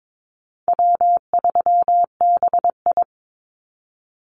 Morse code, heard as a keyed tone, about 22 wpm, 700 Hz. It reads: W3BI
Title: Easy Call Signs Repeated 3x - 22wpm